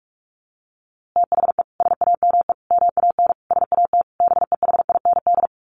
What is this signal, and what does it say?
Morse code, heard as a keyed tone, about 44 wpm, 700 Hz. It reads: THE SAME MAN SAT BEHIND
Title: A Sound Of Thunder 45wpm